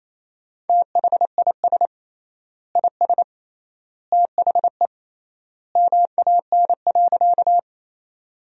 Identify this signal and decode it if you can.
Morse code, heard as a keyed tone, about 27 wpm, 700 Hz. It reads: THIS IS THE MAN.
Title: 3x Word-Spacing - Sentences from Top 200 Words - 28wpm